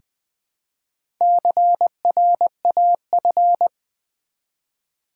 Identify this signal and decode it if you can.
Morse code, heard as a keyed tone, about 20 wpm, 700 Hz. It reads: CRAF
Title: Build A Word - 1 Letter At A Time - 5-Letter Words - 20wpm